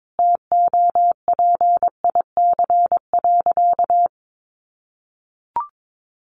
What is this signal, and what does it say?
Morse code, heard as a keyed tone, about 22 wpm, 700 Hz. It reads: TOPIC.
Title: Sentences from Top 1200 Words 22wpm